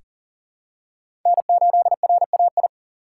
Morse code, heard as a keyed tone, about 37 wpm, 700 Hz. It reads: N8RAI
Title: USA Call Signs - 5 Characters - 2x Character Spacing - 40wpm